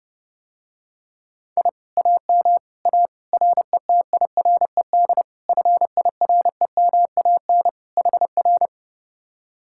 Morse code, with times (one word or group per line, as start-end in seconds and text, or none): 1.6-1.7 I
2.0-2.6 AM
2.9-3.1 A
3.3-5.2 RETIRED
5.5-7.7 FIREMAN
8.0-8.7 HR